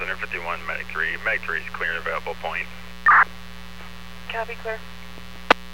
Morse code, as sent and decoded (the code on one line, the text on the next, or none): none